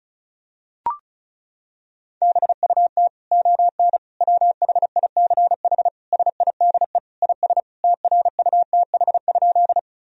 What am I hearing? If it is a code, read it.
Morse code, heard as a keyed tone, about 36 wpm, 700 Hz. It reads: BUT ON WHICH SIDE IS TRUTH?
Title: Sentences from Top 1000 Words - Review Part 1 - 35wpm